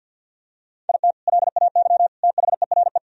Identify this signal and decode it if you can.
Morse code, heard as a keyed tone, about 46 wpm, 700 Hz. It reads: IT LAY THERE
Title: Sentences from Top 1000 Words - Review Part 3 - 50wpm